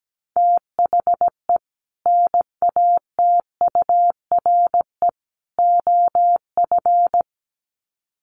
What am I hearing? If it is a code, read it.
Morse code, heard as a keyed tone, about 17 wpm, 700 Hz. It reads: THE NATURE OF
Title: Sets of 3 Words 17wpm